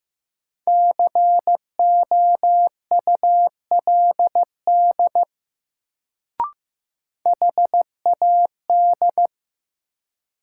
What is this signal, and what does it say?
Morse code, heard as a keyed tone, about 15 wpm, 700 Hz. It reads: COULD HAD
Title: Sets of 2 Words - New Word Drill - 15wpm